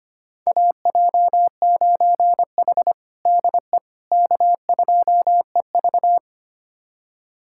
Morse code, heard as a keyed tone, about 25 wpm, 700 Hz. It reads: AJ9H DE K2EV
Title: USA Callsign DE Callsign - 4 Characters - 25wpm